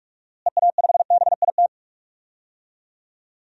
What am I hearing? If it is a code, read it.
Morse code, heard as a keyed tone, about 46 wpm, 700 Hz. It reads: EA5BIT
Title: DXCC-100 Call Signs - 6 Characters - Repeated 2x - 45wpm